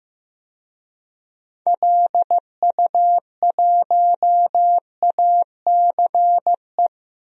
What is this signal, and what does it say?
Morse code, heard as a keyed tone, about 15 wpm, 700 Hz. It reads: LU1ACE
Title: DXCC-100 Call Signs - 6 Characters - 15wpm